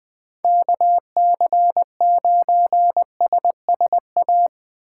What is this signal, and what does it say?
Morse code, heard as a keyed tone, about 23 wpm, 700 Hz. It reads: KC9SSA
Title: Speed Racing - USA Call Signs 20wpm